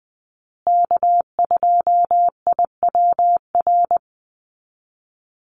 Morse code, heard as a keyed tone, about 20 wpm, 700 Hz. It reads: K2IWR
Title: USA Call Signs Repeated 3x 20wpm